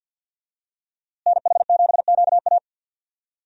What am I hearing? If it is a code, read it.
Morse code, heard as a keyed tone, about 52 wpm, 700 Hz. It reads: NH7QA
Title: USA Call Signs - 5 Characters - Repeated 2x - 50wpm